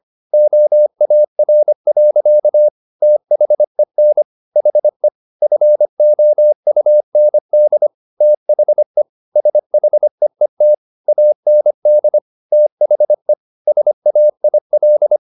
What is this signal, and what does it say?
Morse code, heard as a keyed tone, about 25 wpm, 600 Hz. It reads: OAR. THEN HE FOUND THE SHEET AND THE SAIL